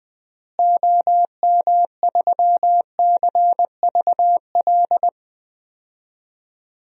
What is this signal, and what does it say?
Morse code, heard as a keyed tone, about 20 wpm, 700 Hz. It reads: OM3CVL
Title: DXCC-100 Call Signs - 6 Characters - Repeated 3x - 20wpm